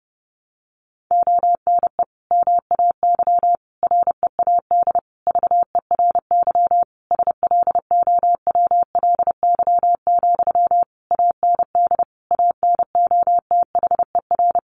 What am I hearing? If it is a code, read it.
Morse code, heard as a keyed tone, about 30 wpm, 700 Hz. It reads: ONE MAY READ VERY SLOWLY, AND ANOTHER
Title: Sentences from Top 1100 Words 30wpm